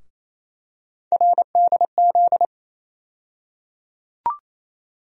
Morse code, heard as a keyed tone, about 28 wpm, 700 Hz. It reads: RDZ
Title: USA Call Sign Postfixes - 3 Letters 28wpm